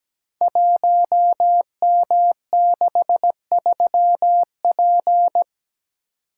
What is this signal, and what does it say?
Morse code, heard as a keyed tone, about 17 wpm, 700 Hz. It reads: 1M63P
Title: Code Groups - 5 Letter:Numbers - 17wpm